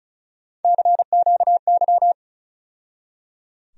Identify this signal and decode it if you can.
Morse code, heard as a keyed tone, about 36 wpm, 700 Hz. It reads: CQY
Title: USA Call Sign Postfixes - 3 Letters 35wpm